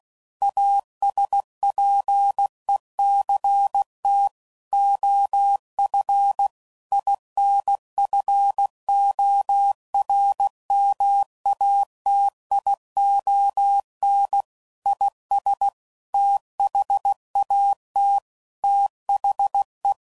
Morse code, heard as a keyed tone, about 16 wpm, 800 Hz. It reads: ASPECT OF INFORMATION IS THAT THE
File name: CW Code Groups @15WPM/Technical Subjects